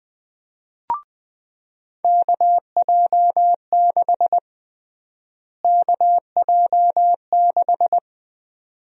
Morse code, heard as a keyed tone, about 20 wpm, 700 Hz. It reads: KJ6 KJ6
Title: USA Call Sign Prefixes - Repeated 2x - 20wpm